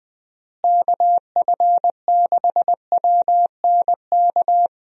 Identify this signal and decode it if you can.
Morse code, heard as a keyed tone, about 20 wpm, 700 Hz. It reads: KF6WNK
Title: USA Call Signs - 6 Characters - Repeated 3x - 20wpm